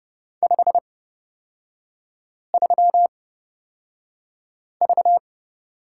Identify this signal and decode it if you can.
Morse code, heard as a keyed tone, about 30 wpm, spaced as for 5 wpm, 700 Hz. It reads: 53V
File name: course / 30:10wpm / 236 Lesson 38e - Letter q - Three Characters 30wpm